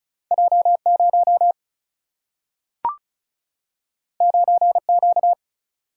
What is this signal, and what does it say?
Morse code, heard as a keyed tone, about 36 wpm, 700 Hz. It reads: J 0 9 Q
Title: Single-Letter Single-Number 35wpm